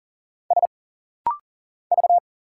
Morse code, heard as a keyed tone, about 40 wpm, 700 Hz. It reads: S V
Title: Single Letters - Rapid-Fire 40wpm